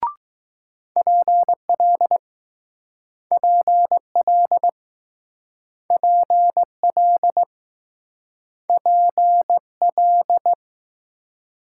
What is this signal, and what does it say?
Morse code, heard as a keyed tone, about 20 wpm, 700 Hz. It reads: PL PL PL PL